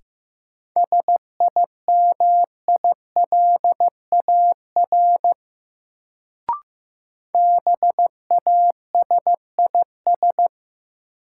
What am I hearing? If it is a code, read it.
Morse code, heard as a keyed tone, about 15 wpm, 700 Hz. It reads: SIMILAR BASIS
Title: Sentences from Top 1000 Words - Review Part 5 - New Word Drill - 15wpm